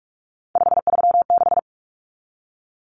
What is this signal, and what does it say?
Morse code, heard as a keyed tone, about 44 wpm, 700 Hz. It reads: 536
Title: Numbers - 3-Digit 45wpm